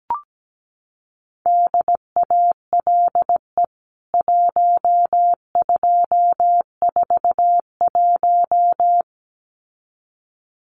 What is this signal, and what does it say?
Morse code, heard as a keyed tone, about 17 wpm, 700 Hz. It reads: DALE 1241
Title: CWT Exchange 17wpm